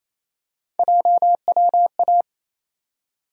Morse code, heard as a keyed tone, about 28 wpm, 700 Hz. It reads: JWA
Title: USA Call Sign Postfixes - 3 Letters 28wpm